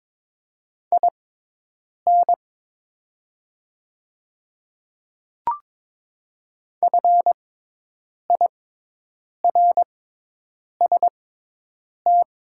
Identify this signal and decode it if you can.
Morse code, heard as a keyed tone, about 22 wpm, spaced as for 8 wpm, 700 Hz. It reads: IN FIRST
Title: 6x Character Spacing - Top 300 Words - 22wpm